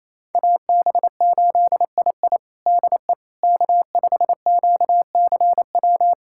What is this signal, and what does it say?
Morse code, heard as a keyed tone, about 28 wpm, 700 Hz. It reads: AB8II DE K5QCW
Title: USA Callsign DE Callsign - 5 Characters - 28wpm